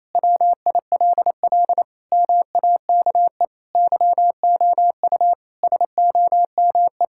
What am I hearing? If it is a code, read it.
Morse code, heard as a keyed tone, about 28 wpm, 700 Hz. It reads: WILL MAKE YOU SOME
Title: Sentences from Top 1000 Words - Review Part 6 - 28wpm